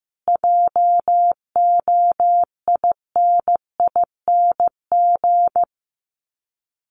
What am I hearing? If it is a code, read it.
Morse code, heard as a keyed tone, about 15 wpm, 700 Hz. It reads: JOINING
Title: Sets of 5 Words Encore2 15wpm